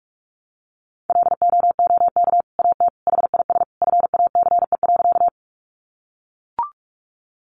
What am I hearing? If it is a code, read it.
Morse code, heard as a keyed tone, about 44 wpm, 700 Hz. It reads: LOOK AT HIS FACE.
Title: Sentences from Top 300 Words 45wpm